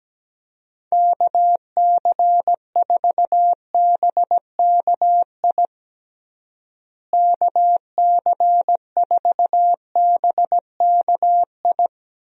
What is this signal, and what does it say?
Morse code, heard as a keyed tone, about 17 wpm, 700 Hz. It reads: KC4BKI KC4BKI
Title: USA Call Signs - 6 Characters - Repeated 3x - 17wpm